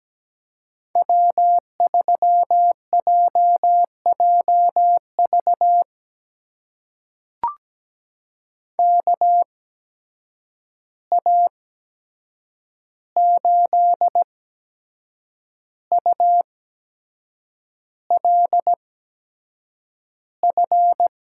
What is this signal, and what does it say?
Morse code, heard as a keyed tone, about 17 wpm, spaced as for 7 wpm, 700 Hz. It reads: W3JJV KA8ULF
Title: USA Call Signs - 8x Character Spacing - 17wpm